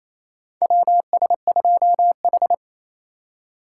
Morse code, heard as a keyed tone, about 28 wpm, 700 Hz. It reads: WS2H
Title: USA Call Signs - 4 Characters - 2x Character Spacing - 28wpm